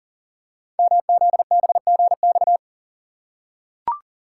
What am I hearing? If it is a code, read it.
Morse code, heard as a keyed tone, about 42 wpm, 690 Hz. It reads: MZBGX
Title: Code Groups - 5 Letters - 40wpm